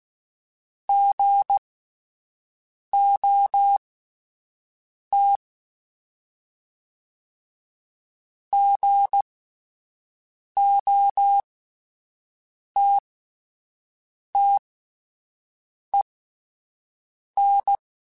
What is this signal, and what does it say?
Morse code, heard as a keyed tone, about 16 wpm, spaced as for 5 wpm, 800 Hz. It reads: GOT GOTTEN
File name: Short Runs / Short Runs-mp3s / Gotten @10